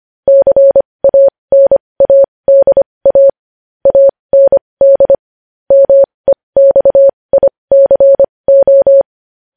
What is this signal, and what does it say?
Morse code, heard as a keyed tone, about 25 wpm, spaced as for 20 wpm, 550 Hz. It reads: CANADA AND MEXICO
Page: News Headlines in Morse Code at 20 WPM